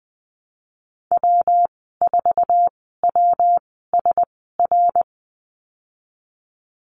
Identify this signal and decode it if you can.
Morse code, heard as a keyed tone, about 20 wpm, spaced as for 14 wpm, 700 Hz. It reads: W4WSR
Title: 128 Lesson 21g - Number 6 - Call Signs 20wpm